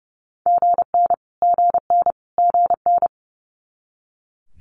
Morse code, heard as a keyed tone, about 30 wpm, 700 Hz. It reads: GN GN GN
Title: Multi-letter Phonemes Repeated 3x 30wpm